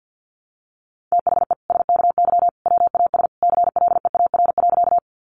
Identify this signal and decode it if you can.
Morse code, heard as a keyed tone, about 50 wpm, 700 Hz. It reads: THE SKY WAS CLEAR.
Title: Sentences from Top 1200 Words 50wpm